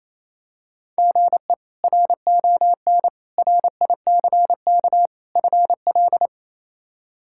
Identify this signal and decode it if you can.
Morse code, heard as a keyed tone, about 28 wpm, 700 Hz. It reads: GE RON RICK FL
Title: K1USN SST QSO Practice - 28wpm